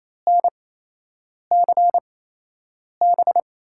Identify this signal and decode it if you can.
Morse code, heard as a keyed tone, about 27 wpm, spaced as for 8 wpm, 700 Hz. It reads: NCB
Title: USA Call Signs - 5 Characters - 8x Character Spacing - 28wpm